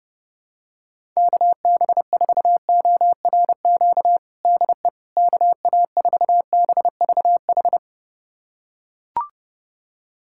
Morse code, heard as a keyed tone, about 30 wpm, 700 Hz. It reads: KB4ORQ DE KA4BVH
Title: USA Callsign DE Callsign - 6 Characters - 30wpm